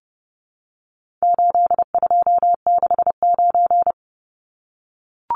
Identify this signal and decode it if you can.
Morse code, heard as a keyed tone, about 30 wpm, 700 Hz. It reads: 8269
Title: Numbers - 4-Digit 30wpm